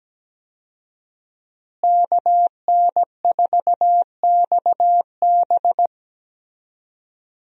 Morse code, heard as a keyed tone, about 17 wpm, 700 Hz. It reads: KN4XB
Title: USA Call Signs - 5 Characters - Repeated 3x - 17wpm